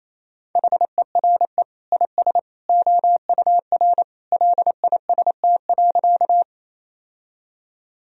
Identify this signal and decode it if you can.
Morse code, heard as a keyed tone, about 28 wpm, 700 Hz. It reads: HERE IS OUR LIST.
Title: Sentences from Top 1000 Words - Review Part 2 - 28wpm